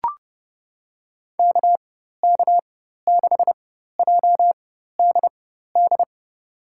Morse code, heard as a keyed tone, about 30 wpm, spaced as for 14 wpm, 700 Hz. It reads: KK6JDD
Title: USA Call Signs - 6 Characters - 4x Character Spacing - 30wpm